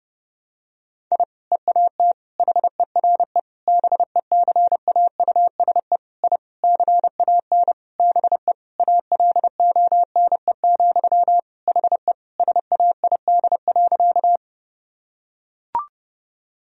Morse code, heard as a keyed tone, about 32 wpm, 700 Hz. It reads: I EAT HERE BECAUSE I CAN BE ALONE, HE SAID.
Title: Sentences from Top 1000 Words - Review Part 3 - 30wpm